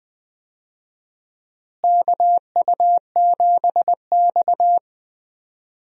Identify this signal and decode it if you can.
Morse code, heard as a keyed tone, about 20 wpm, 700 Hz. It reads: KU7X
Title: USA Call Signs - 4 Characters - Repeated 2x - 20wpm